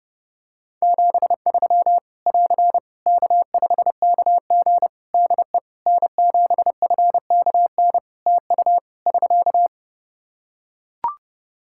Morse code, heard as a keyed tone, about 30 wpm, 700 Hz. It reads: 73 <AR> K5KG DE N7FKN TU <SK>
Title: K1USN SST QSO Practice 30wpm